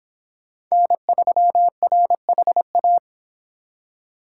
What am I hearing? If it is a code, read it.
Morse code, heard as a keyed tone, about 25 wpm, 700 Hz. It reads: N3RHA